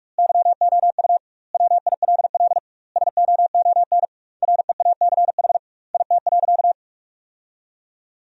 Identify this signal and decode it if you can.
Morse code, heard as a keyed tone, about 46 wpm, 700 Hz. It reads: YOU WILL SOON REACH IT.
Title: 2x Word-Spacing - Sentences from Top 400 Words - 45wpm